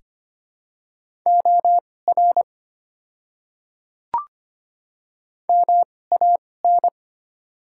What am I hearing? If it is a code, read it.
Morse code, heard as a keyed tone, about 25 wpm, spaced as for 18 wpm, 700 Hz. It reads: OR MAN
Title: 2x Character Spacing - Top 200 Words - 25wpm